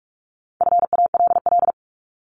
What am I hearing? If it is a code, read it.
Morse code, heard as a keyed tone, about 45 wpm, 700 Hz. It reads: FALL
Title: Top 500 Words 45wpm